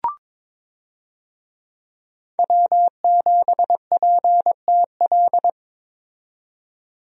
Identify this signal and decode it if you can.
Morse code, heard as a keyed tone, about 21 wpm, 700 Hz. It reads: W7PTL
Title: USA Call Signs - 5 Characters - 22wpm